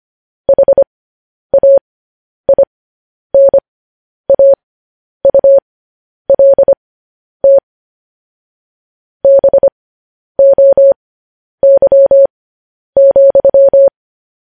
Morse code, heard as a keyed tone, about 25 wpm, spaced as for 10 wpm, 550 Hz. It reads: HAINAULT BOY,
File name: podcast / News headlines-1x-25x10